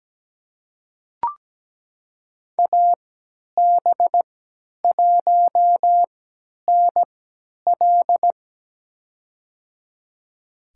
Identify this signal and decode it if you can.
Morse code, heard as a keyed tone, about 17 wpm, spaced as for 10 wpm, 700 Hz. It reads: AB1NL
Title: USA Call Signs - 3x Character Spacing - 17wpm